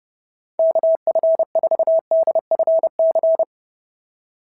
Morse code, heard as a keyed tone, about 26 wpm, 650 Hz. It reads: KF4DFC
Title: SOTA QSO Practice 30wpm